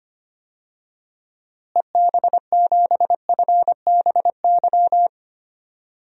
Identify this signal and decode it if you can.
Morse code, heard as a keyed tone, about 25 wpm, 700 Hz. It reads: EB7FBY
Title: DXCC-100 Call Signs - 6 Characters - Repeated 3x - 25wpm